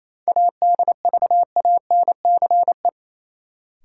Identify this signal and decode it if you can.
Morse code, heard as a keyed tone, about 28 wpm, 700 Hz. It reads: ADVANCE
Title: Sentences from Top 1400 Words - New Word Drill - 28wpm